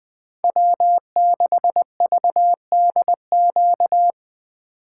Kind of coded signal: Morse code

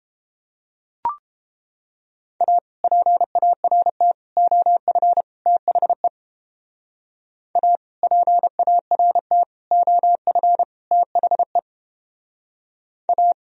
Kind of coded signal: Morse code

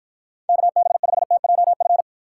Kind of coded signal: Morse code

